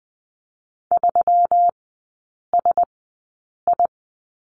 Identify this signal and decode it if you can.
Morse code, heard as a keyed tone, about 20 wpm, spaced as for 8 wpm, 700 Hz. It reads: 3SI
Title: 127 Lesson 21e - Number 6 - Three Characters 20wpm